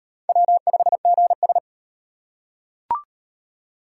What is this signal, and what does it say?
Morse code, heard as a keyed tone, about 35 wpm, 700 Hz. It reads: W5GS